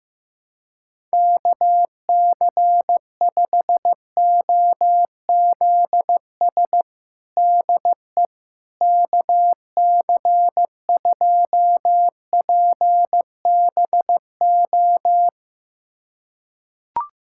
Morse code, heard as a keyed tone, about 16 wpm, 700 Hz. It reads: KC5OZS DE KC2PBO
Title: USA Callsign DE Callsign 15wpm